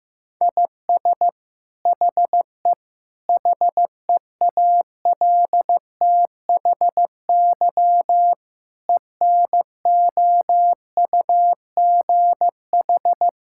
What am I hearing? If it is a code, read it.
Morse code, heard as a keyed tone, about 15 wpm, 700 Hz. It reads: IS HE HEALTHY ENOUGH
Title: Sentences from Top 1000 Words - Review Part 5 - 15wpm